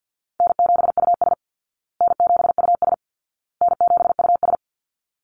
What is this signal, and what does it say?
Morse code, heard as a keyed tone, about 49 wpm, 700 Hz. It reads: N7US N7US N7US
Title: Call Signs Repeated 2x - 50wpm